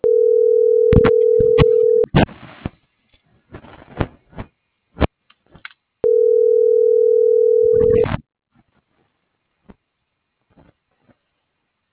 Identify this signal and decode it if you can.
Ringback tone (North America), 440 + 480 Hz.